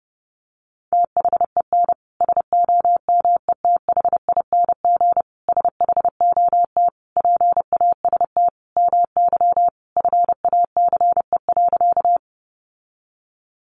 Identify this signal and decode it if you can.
Morse code, heard as a keyed tone, about 30 wpm, 700 Hz. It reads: THEN SOMETHING SHOT PAST MY FACE.
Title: Sentences from Top 1100 Words 30wpm